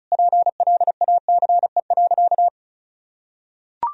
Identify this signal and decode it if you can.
Morse code, heard as a keyed tone, about 36 wpm, 700 Hz. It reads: PLACE.
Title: Sentences from Top 1000 Words - Review Part 1 - 35wpm